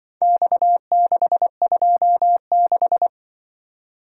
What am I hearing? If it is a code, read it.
Morse code, heard as a keyed tone, about 23 wpm, 700 Hz. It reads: X626
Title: Speed-Racing - Four Letter-Number Combinations - 20wpm